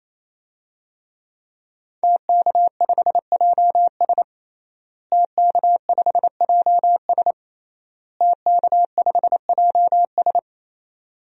Morse code, heard as a keyed tone, about 28 wpm, 700 Hz. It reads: TK5JS TK5JS TK5JS
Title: DXCC-100 Call Signs - 5 Characters - Repeated 3x - 28wpm